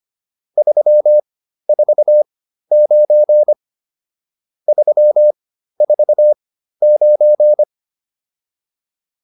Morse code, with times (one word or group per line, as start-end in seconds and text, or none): 0.6-3.5 349
4.7-7.6 349